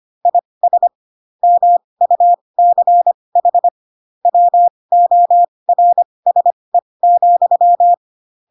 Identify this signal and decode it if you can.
Morse code, heard as a keyed tone, about 25 wpm, spaced as for 20 wpm, 700 Hz. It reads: IS MUCH WORSE,
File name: Pinocchio / Pinocchio ch4 15 25 ik1hge